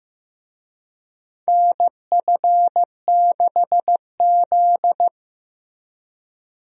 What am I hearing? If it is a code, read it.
Morse code, heard as a keyed tone, about 15 wpm, 700 Hz. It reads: NF6Z
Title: USA Call Signs - 4 Characters - Repeated 3x - 15wpm